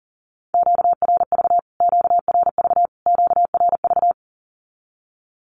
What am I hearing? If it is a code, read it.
Morse code, heard as a keyed tone, about 40 wpm, 700 Hz. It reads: QRV QRV QRV
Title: Q-Codes Repeated 3x 40wpm